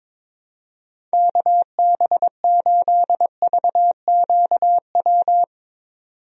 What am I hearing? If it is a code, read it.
Morse code, heard as a keyed tone, about 22 wpm, 700 Hz. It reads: KB8VQW
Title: USA Call Signs - 6 Characters - 22wpm